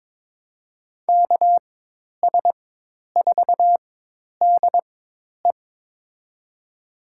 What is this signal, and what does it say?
Morse code, heard as a keyed tone, about 21 wpm, spaced as for 10 wpm, 700 Hz. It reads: KS4DE
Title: USA Call Signs - 4x Character Spacing - 22wpm